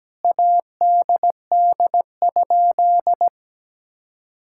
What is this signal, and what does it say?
Morse code, heard as a keyed tone, about 17 wpm, 700 Hz. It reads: ADD?
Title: Sentences from Top 1000 Words - Review Part 5 - 17wpm